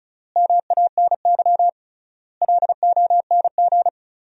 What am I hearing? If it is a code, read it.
Morse code, heard as a keyed tone, about 35 wpm, 700 Hz. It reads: MANY LONG
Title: 3x Word-Spacing - Sentences from Top 200 Words - 35wpm